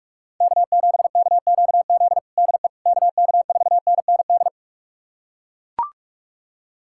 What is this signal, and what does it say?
Morse code, heard as a keyed tone, about 41 wpm, 700 Hz. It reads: K7KQZ DE KK4NND
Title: USA Callsign DE Callsign 45wpm